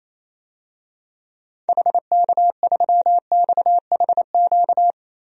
Morse code, heard as a keyed tone, about 28 wpm, 700 Hz. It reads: HK3XHQ
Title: DXCC-100 Call Signs - 6 Characters - 28wpm